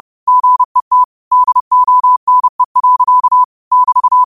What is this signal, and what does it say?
Morse code, heard as a keyed tone, about 30 wpm, 1000 Hz. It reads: GET DONE. <BT>